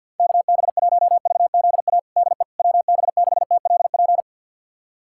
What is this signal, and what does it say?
Morse code, heard as a keyed tone, about 52 wpm, 700 Hz. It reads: KB1VZA DE WB6TLP
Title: USA Callsign DE Callsign - 6 Characters - 50wpm